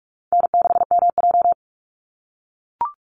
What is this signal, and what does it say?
Morse code, heard as a keyed tone, about 44 wpm, 700 Hz. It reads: N6MJ
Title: Easy Call Signs - 45wpm